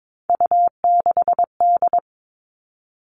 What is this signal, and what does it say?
Morse code, heard as a keyed tone, about 22 wpm, 700 Hz. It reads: U6D
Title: All Cut Numbers - 3-Digit 22wpm